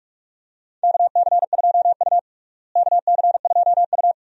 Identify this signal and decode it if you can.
Morse code, heard as a keyed tone, about 42 wpm, 700 Hz. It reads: KC2U KC2U
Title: USA Call Signs - 4 Characters - Repeated 2x - 45wpm